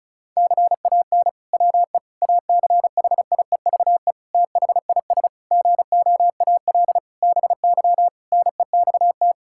Morse code, heard as a keyed tone, about 36 wpm, 700 Hz. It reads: CAN WE ACHIEVE THIS GOAL BY NEXT
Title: Sentences from Top 1000 Words - Review Part 5 - 35wpm